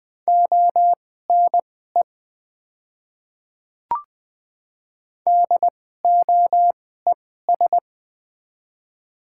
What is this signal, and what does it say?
Morse code, heard as a keyed tone, about 20 wpm, spaced as for 14 wpm, 700 Hz. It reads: ONE DOES
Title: 2x Character Spacing - Top 500 Words - 20wpm